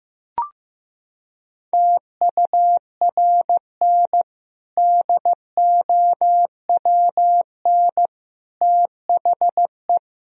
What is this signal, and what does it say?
Morse code, heard as a keyed tone, about 15 wpm, 700 Hz. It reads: TURN DOWN THE
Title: Sentences from Top 1000 Words - Review Part 3 - 15wpm